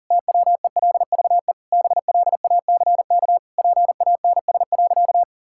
Morse code, heard as a keyed tone, about 40 wpm, 700 Hz. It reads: TWELVE BLACK PANS.
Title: Pangram Challenge v2 - 40wpm